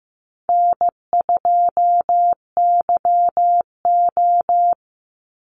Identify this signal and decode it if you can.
Morse code, heard as a keyed tone, about 15 wpm, 700 Hz. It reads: N2YO
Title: Easy Call Signs Repeated 3x - 15wpm